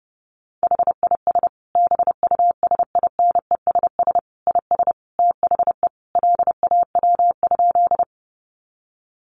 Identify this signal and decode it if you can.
Morse code, heard as a keyed tone, about 30 wpm, 700 Hz. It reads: HIS BUSINESS IS THE LAW?
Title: Sentences from Top 600 Words 30wpm